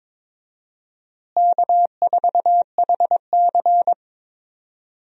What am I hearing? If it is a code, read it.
Morse code, heard as a keyed tone, about 21 wpm, 700 Hz. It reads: K4HC
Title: USA Call Signs - 4 Characters - Repeated 2x - 22wpm